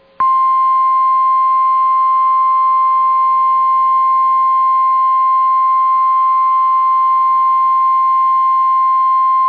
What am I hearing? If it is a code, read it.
Emergency Alert System (SAME) tones (the 1050 Hz attention tone). It is heard from 1.1 to 9.0 s.